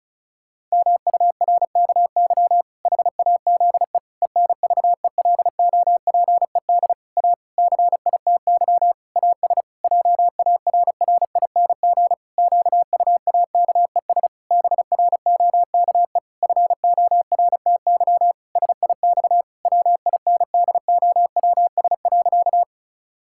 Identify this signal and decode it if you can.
Morse code, heard as a keyed tone, about 36 wpm, 700 Hz. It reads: MURKY HAZE ENVELOPED A CITY AS JARRING QUAKES BROKE FORTY SIX WINDOWS.
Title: Pangram Challenge v2 - 35wpm